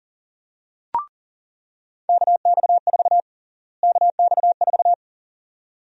Morse code, heard as a keyed tone, about 40 wpm, 700 Hz. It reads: KX4 KX4
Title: USA Call Sign Prefixes - Repeated 2x - 40wpm